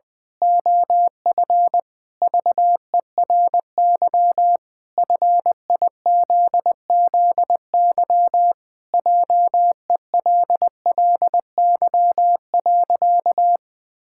Morse code, heard as a keyed tone, about 20 wpm, 700 Hz. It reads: OF VERY FIZZY JELLY.
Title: Pangram Challenge v2 - 20wpm